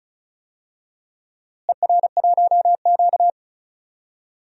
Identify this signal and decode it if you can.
Morse code, heard as a keyed tone, about 36 wpm, 700 Hz. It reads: ER1Q
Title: DXCC-100 Call Signs - 4 Characters - Repeated 2x - 35wpm